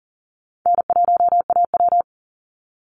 Morse code, heard as a keyed tone, about 40 wpm, 700 Hz. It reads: N1AW
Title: NA SOTA Chaser Callsigns 40wpm